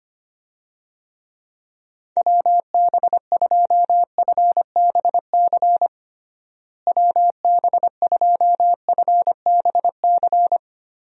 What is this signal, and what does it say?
Morse code, heard as a keyed tone, about 25 wpm, 700 Hz. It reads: WB2FBC WB2FBC
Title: USA Call Signs - 6 Characters - Repeated 3x - 25wpm